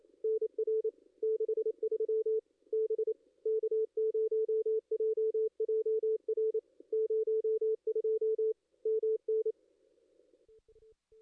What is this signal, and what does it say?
Morse code, heard as a keyed tone, about 25 wpm, 440 Hz. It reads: NR 63 B K0JJR 02 MN
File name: contests / ss-cw-k5tr-op-n5rz -2300